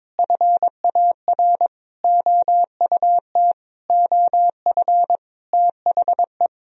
Morse code, heard as a keyed tone, about 22 wpm, 700 Hz. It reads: FAR OUT OF THE
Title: Sentences from Top 1000 Words - Review Part 1 - 22wpm